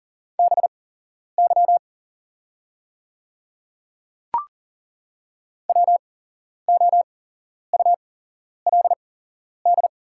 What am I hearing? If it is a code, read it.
Morse code, heard as a keyed tone, about 40 wpm, spaced as for 11 wpm, 690 Hz. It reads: BY WOULD
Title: 8x Character Spacing - Top 100 Words - 40wpm